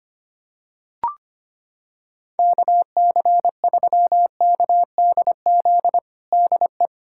Morse code, heard as a keyed tone, about 25 wpm, 700 Hz. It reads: KC3KDZ DE
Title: USA Callsign DE Callsign - 6 Characters - 25wpm